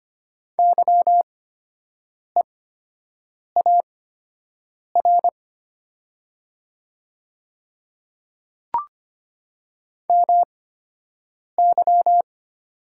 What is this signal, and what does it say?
Morse code, heard as a keyed tone, about 25 wpm, spaced as for 7 wpm, 700 Hz. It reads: YEAR MY
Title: 8x Character Spacing - Top 200 Words - 25wpm